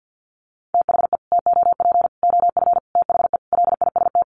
Morse code, heard as a keyed tone, about 48 wpm, 700 Hz. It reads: THE TOP OF THE LIST
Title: Sets of 5 Words Encore2 50wpm